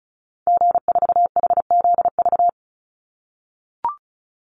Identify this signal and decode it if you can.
Morse code, heard as a keyed tone, about 34 wpm, 700 Hz. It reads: G4HZV
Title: Call Signs - 35wpm